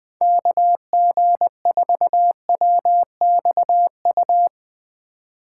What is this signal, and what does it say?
Morse code, heard as a keyed tone, about 20 wpm, 700 Hz. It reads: KG4WXU
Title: 2023 POTA Hunter Call Signs - 20wpm